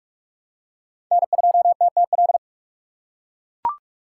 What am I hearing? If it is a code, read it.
Morse code, heard as a keyed tone, about 46 wpm, 690 Hz. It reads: N2TTL
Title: USA Call Signs - 5 Characters - 4x Character Spacing - 45wpm